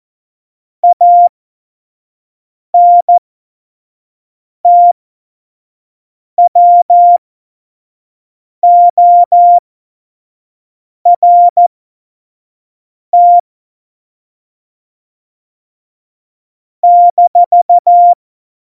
Morse code, heard as a keyed tone, about 14 wpm, spaced as for 5 wpm, 700 Hz. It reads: ANTWORT -
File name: Morsekurs / Witze / Witz 30. Oktober 2018, 25 F70